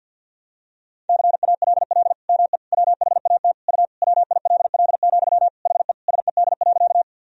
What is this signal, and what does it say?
Morse code, heard as a keyed tone, about 46 wpm, 700 Hz. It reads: CALL ME WHAT U WILL, HE SED.
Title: Sentences from Top 200 Words and Abbreviations 50wpm